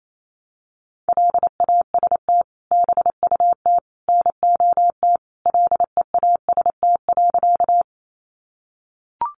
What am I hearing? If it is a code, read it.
Morse code, heard as a keyed tone, about 28 wpm, 700 Hz. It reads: LAST BUT NOT LEAST.
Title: Sentences from Top 700 Words 28wpm